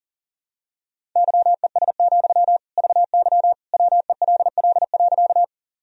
Morse code, heard as a keyed tone, about 40 wpm, 700 Hz. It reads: YES, VY WELL.
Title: Sentences from Top 500 Words and Abbreviations 40wpm